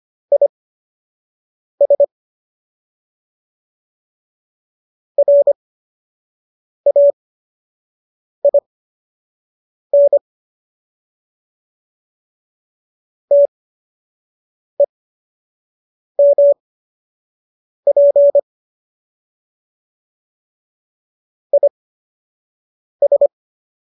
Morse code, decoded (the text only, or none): IS RAIN TEMP IS